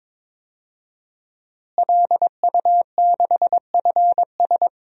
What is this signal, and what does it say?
Morse code, heard as a keyed tone, about 22 wpm, 700 Hz. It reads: LU6FS
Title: DXCC-100 Call Signs - 5 Characters - Repeated 2x - 22wpm